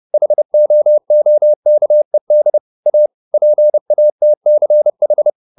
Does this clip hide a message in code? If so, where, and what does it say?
Yes: Morse code from 0.1 to 5.3 s, heard as a keyed tone, about 30 wpm, 600 Hz. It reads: HOOKED A PATCH